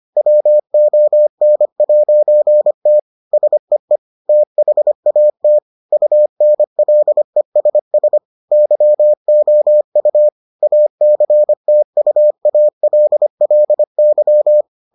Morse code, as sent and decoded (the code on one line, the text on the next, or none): .-- --- -. .----. - / ... . / - .... .- - / ..- -. .-.. ... ... / -.-- --- ..- / .- -.-. - ..- .- .-.. .-.. -.--
WON'T SEE THAT UNLESS YOU ACTUALLY